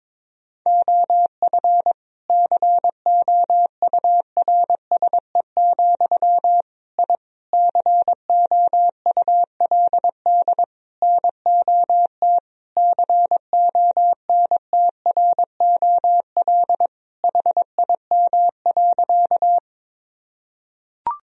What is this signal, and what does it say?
Morse code, heard as a keyed tone, about 22 wpm, 700 Hz. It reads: OF COURSE, I COULD NOT CONTROL HIM.
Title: Sentences from Top 1000 Words - Review Part 3 - 22wpm